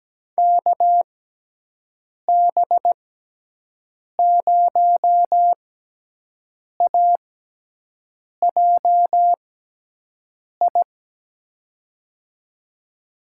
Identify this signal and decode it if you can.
Morse code, heard as a keyed tone, about 17 wpm, spaced as for 6 wpm, 700 Hz. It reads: KB0AJI
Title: USA Call Signs - 6 Characters - 6x Character Spacing - 17wpm